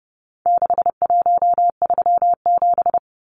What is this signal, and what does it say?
Morse code, heard as a keyed tone, about 30 wpm, 700 Hz. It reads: 6137
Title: Numbers - 4-Digit 30wpm